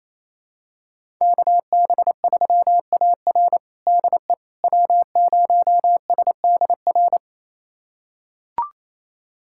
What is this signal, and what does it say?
Morse code, heard as a keyed tone, about 28 wpm, 700 Hz. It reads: KB3AR DE W0SDR
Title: USA Callsign DE Callsign - 5 Characters - 28wpm